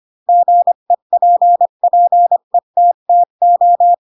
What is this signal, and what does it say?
Morse code, heard as a keyed tone, about 25 wpm, 700 Hz. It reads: GEPPETTO
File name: Pinocchio / Pinocchio ch7 20 25 ik1hge